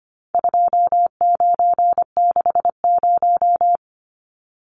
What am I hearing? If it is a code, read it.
Morse code, heard as a keyed tone, about 25 wpm, 700 Hz. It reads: 2960
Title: Numbers - 4-Digit 25wpm